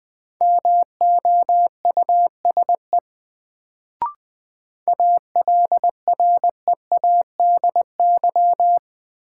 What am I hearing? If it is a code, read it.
Morse code, heard as a keyed tone, about 20 wpm, 700 Hz. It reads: MOUSE ALREADY
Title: Sentences from Top 700 Words - New Word Drill - 20wpm